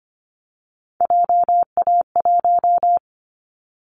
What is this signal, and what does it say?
Morse code, heard as a keyed tone, about 25 wpm, 700 Hz. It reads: JA1
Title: Callsign Prefixes for Top 50 DXCC - 25wpm